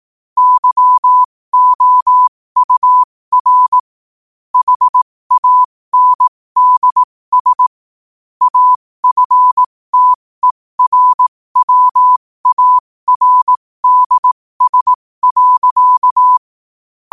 Morse code, heard as a keyed tone, about 18 wpm, spaced as for 15 wpm, 1000 Hz. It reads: YOUR HANDS AFTERWARDS.
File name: qotdfiles / 15wpm